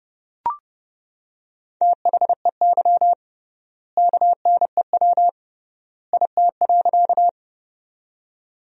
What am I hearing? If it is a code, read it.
Morse code, heard as a keyed tone, about 28 wpm, 700 Hz. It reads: THEY KNEW IT.
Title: 3x Word-Spacing - Sentences from Top 400 Words - 30wpm